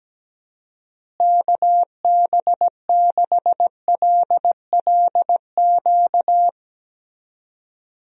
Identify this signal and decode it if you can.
Morse code, heard as a keyed tone, about 17 wpm, 700 Hz. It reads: KB6LLQ
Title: USA Call Signs - 6 Characters - Repeated 3x - 17wpm